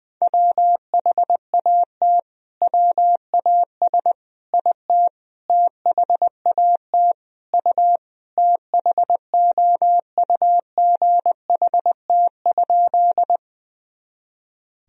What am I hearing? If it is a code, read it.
Morse code, heard as a keyed tone, about 20 wpm, 700 Hz. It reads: WHAT WAS IT THAT U THOUGHT?
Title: Sentences from Top 300 Words and Abbreviations 20wpm